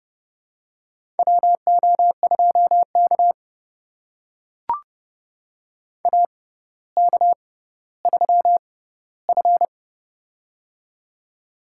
Morse code, heard as a keyed tone, about 31 wpm, spaced as for 15 wpm, 700 Hz. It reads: WO2K AK3F
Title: USA Call Signs - 4 Characters - 6x Character Spacing - 30wpm